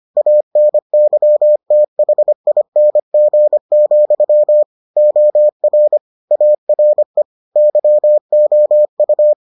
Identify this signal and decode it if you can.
Morse code, heard as a keyed tone, about 25 wpm, 600 Hz. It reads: ANYTHING, OR ARE YOU